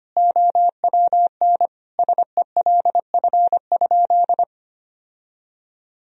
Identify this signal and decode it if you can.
Morse code, heard as a keyed tone, about 25 wpm, 700 Hz. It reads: OWN SELF?
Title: Sentences from Top 1000 Words - Review Part 1 - 25wpm